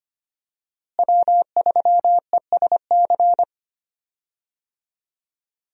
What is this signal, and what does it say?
Morse code, heard as a keyed tone, about 25 wpm, 690 Hz. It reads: W3ESC
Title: USA Call Signs - 5 Characters - Repeated 2x - 25wpm